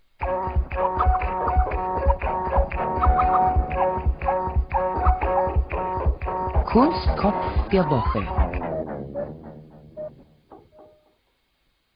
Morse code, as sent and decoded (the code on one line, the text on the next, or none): none